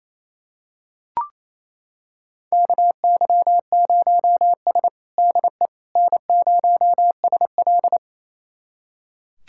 Morse code, heard as a keyed tone, about 27 wpm, 700 Hz. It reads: KY0S DE N0SL
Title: USA Callsign DE Callsign - 4 Characters - 28wpm